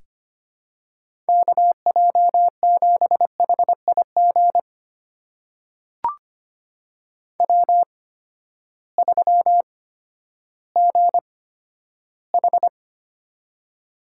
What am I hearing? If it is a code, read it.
Morse code, heard as a keyed tone, about 25 wpm, spaced as for 12 wpm, 700 Hz. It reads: KJ7HIG W3GH
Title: USA Call Signs - 8x Character Spacing - 25wpm